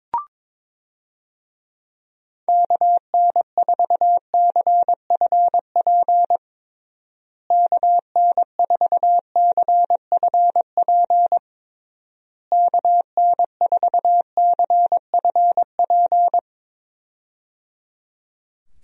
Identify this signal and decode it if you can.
Morse code, heard as a keyed tone, about 21 wpm, 700 Hz. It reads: KN4CFP KN4CFP KN4CFP
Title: USA Call Signs - 6 Characters - Repeated 3x - 22wpm